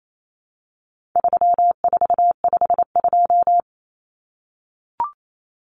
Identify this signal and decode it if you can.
Morse code, heard as a keyed tone, about 28 wpm, 700 Hz. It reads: 3452
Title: Numbers - 4-Digit 28wpm